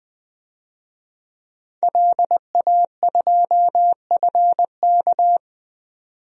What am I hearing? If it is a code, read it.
Morse code, heard as a keyed tone, about 20 wpm, 700 Hz. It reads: LA2FK
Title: DXCC-100 Call Signs - 5 Characters - 20wpm